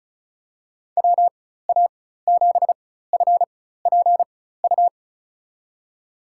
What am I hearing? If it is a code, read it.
Morse code, heard as a keyed tone, about 36 wpm, spaced as for 16 wpm, 700 Hz. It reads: WA7FPU
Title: USA Call Signs - 6 Characters - 4x Character Spacing - 35wpm